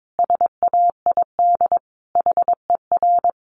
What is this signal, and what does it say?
Morse code, heard as a keyed tone, about 22 wpm, 700 Hz. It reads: SAID HER
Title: Sentences from Top 1500 Words 22wpm